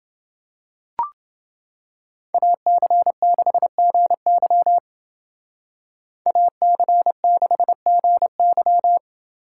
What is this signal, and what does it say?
Morse code, heard as a keyed tone, about 29 wpm, 700 Hz. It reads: AC6GY AC6GY